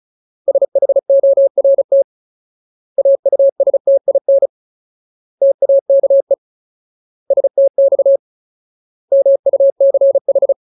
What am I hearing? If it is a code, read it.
Morse code, heard as a keyed tone, about 34 wpm, 550 Hz. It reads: SHORT AUSTIN TAKE STX MUCH
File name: Ultimate Word List / 35WPM Ultimate